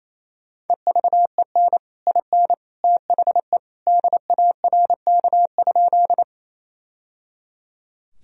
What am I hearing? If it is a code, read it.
Morse code, heard as a keyed tone, about 28 wpm, 700 Hz. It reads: EVEN IN THE DARK?
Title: Sentences from Top 1000 Words - Review Part 1 - 28wpm